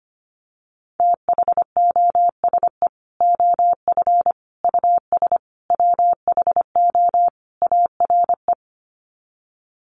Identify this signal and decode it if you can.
Morse code, heard as a keyed tone, about 25 wpm, 700 Hz. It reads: THOSE OF US WHO ARE
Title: Sets of 5 Words Encore2 25wpm